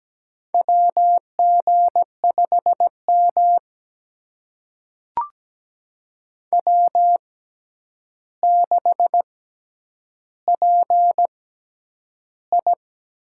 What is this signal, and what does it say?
Morse code, heard as a keyed tone, about 17 wpm, spaced as for 9 wpm, 700 Hz. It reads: WG5M W6PI
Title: USA Call Signs - 4 Characters - 6x Character Spacing - 17wpm